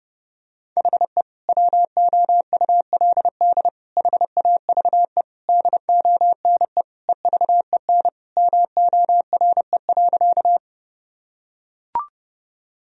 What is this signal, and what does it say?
Morse code, heard as a keyed tone, about 32 wpm, 700 Hz. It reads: HE WOULD HAVE DONE EVEN MORE.
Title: Sentences from Top 1000 Words - Review Part 1 - 30wpm